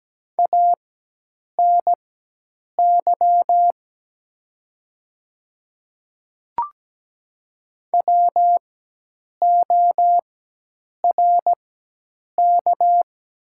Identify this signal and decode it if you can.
Morse code, heard as a keyed tone, about 17 wpm, spaced as for 8 wpm, 700 Hz. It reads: ANY WORK
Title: 4x Character Spacing - Top 300 Words - 17wpm